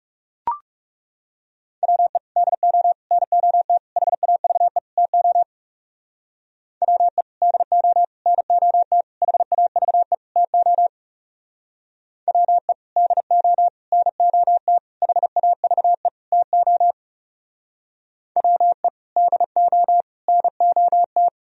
Morse code, heard as a keyed tone, about 36 wpm, 700 Hz. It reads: WE DO NOT HAVE TO WE DO NOT HAVE TO WE DO NOT HAVE TO WE DO NOT
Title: Speed-Racing - Sets of 5 Words 30wpm